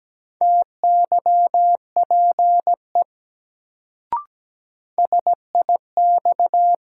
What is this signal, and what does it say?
Morse code, heard as a keyed tone, about 17 wpm, 700 Hz. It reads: TYPE SIX
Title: Sentences from Top 600 Words - New Word Drill - 17wpm